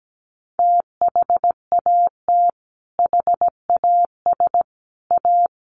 Morse code, heard as a keyed tone, about 17 wpm, 700 Hz. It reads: THAT HAS A
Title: Sentences from Top 900 Words 17wpm